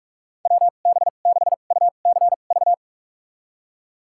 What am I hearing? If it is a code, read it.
Morse code, heard as a keyed tone, about 46 wpm, spaced as for 33 wpm, 700 Hz. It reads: WB6UCV
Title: USA Call Signs - 2x Character Spacing - 45wpm